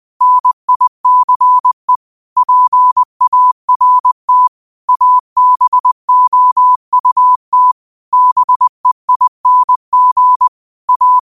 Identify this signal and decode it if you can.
Morse code, heard as a keyed tone, about 20 wpm, 1000 Hz. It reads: NICE PART ABOUT BEING A
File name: qotdfiles / 20wpm